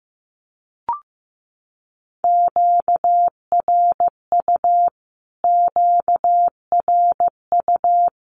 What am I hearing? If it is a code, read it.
Morse code, heard as a keyed tone, about 15 wpm, 700 Hz. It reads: QRU QRU
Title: Q-Codes Repeated 3x 15wpm